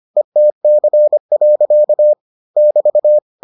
Morse code, heard as a keyed tone, about 25 wpm, 600 Hz. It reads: ETC. <BT>